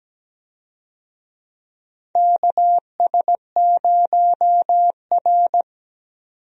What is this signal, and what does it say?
Morse code, heard as a keyed tone, about 17 wpm, 700 Hz. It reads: KS0R
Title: USA Call Signs - 4 Characters - 17wpm